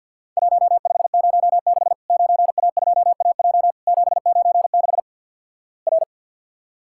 Morse code, heard as a keyed tone, about 50 wpm, 700 Hz. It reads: 1506 9A2AJ 796 R
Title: Worked All Europe DX Contest - QTC Exchange with Repeats - 50wpm